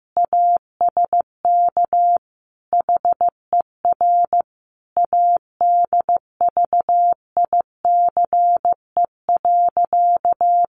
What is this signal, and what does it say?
Morse code, heard as a keyed tone, about 15 wpm, 700 Hz. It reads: ASK HER ADVICE.
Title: Sentences from Top 1100 Words 15wpm